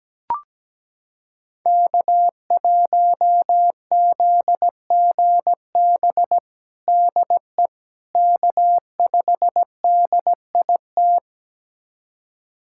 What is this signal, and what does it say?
Morse code, heard as a keyed tone, about 17 wpm, 700 Hz. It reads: K1ZGB DE K5DIT
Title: USA Callsign DE Callsign - 5 Characters - 17wpm